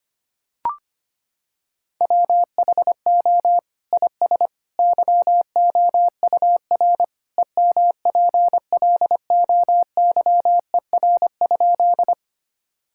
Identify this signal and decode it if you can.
Morse code, heard as a keyed tone, about 25 wpm, 700 Hz. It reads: WHO IS YOUR EMPLOYER?
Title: Sentences from Top 1600 Words 25wpm